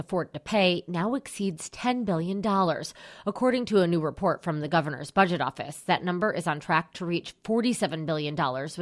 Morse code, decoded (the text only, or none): none